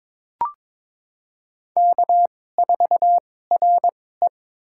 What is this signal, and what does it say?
Morse code, heard as a keyed tone, about 21 wpm, spaced as for 16 wpm, 700 Hz. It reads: K4RE
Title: USA Call Signs - 2x Character Spacing - 22wpm